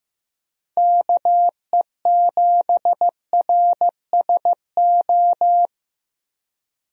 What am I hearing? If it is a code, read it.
Morse code, heard as a keyed tone, about 16 wpm, 700 Hz. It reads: KE7RSO
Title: USA Call Signs - 6 Characters - Repeated 3x - 15wpm